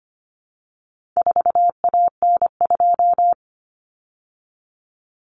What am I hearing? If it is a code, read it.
Morse code, heard as a keyed tone, about 25 wpm, 700 Hz. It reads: 4AN2
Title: Common Cut Numbers - 4-Digit 25wpm